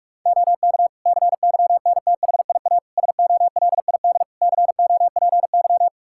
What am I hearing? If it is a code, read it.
Morse code, heard as a keyed tone, about 41 wpm, 700 Hz. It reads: OK CYNTHIA SOLID COPY
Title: K1USN SST QSO Practice 45wpm